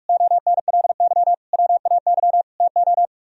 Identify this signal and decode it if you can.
Morse code, heard as a keyed tone, about 44 wpm, 700 Hz. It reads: ONLY WAY TO